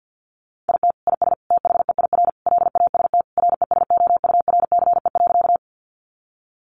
Morse code, heard as a keyed tone, about 50 wpm, 700 Hz. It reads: IT IS THEIR LAST RESOURCE.
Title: Sentences from Top 1300 Words 50wpm